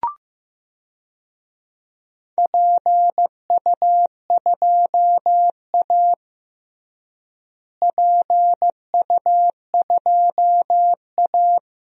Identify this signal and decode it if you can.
Morse code, heard as a keyed tone, about 15 wpm, 700 Hz. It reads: PU2A PU2A
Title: DXCC-100 Call Signs - 4 Characters - Repeated 2x - 15wpm